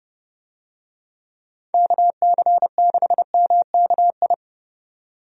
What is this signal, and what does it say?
Morse code, heard as a keyed tone, about 30 wpm, 700 Hz. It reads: KC6MKI
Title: USA Call Signs - 6 Characters - 30wpm